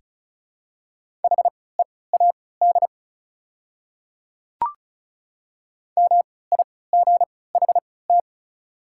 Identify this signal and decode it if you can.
Morse code, heard as a keyed tone, about 36 wpm, spaced as for 20 wpm, 690 Hz. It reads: HEAD MIGHT
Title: 3x Character Spacing - Top 300 Words - 35wpm